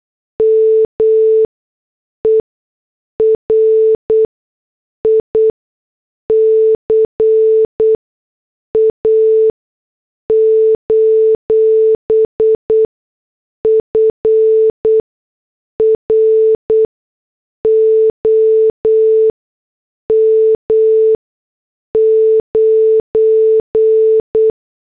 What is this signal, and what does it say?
Morse code, heard as a keyed tone, about 8 wpm, spaced as for 6 wpm, 440 Hz. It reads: MERICA: FROM 9